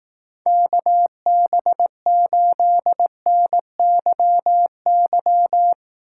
Morse code, heard as a keyed tone, about 18 wpm, 700 Hz. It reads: KB8NYY
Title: Speed Racing - USA 2x3 Call Signs 15wpm